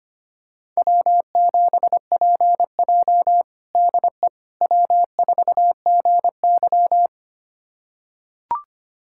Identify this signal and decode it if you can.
Morse code, heard as a keyed tone, about 25 wpm, 700 Hz. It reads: W7PJ DE W4GY
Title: USA Callsign DE Callsign - 4 Characters - 25wpm